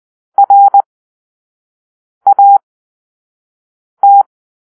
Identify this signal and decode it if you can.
Morse code, heard as a keyed tone, about 20 wpm, spaced as for 5 wpm, 800 Hz. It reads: RAT
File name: arrl / Grp-1 Drill-2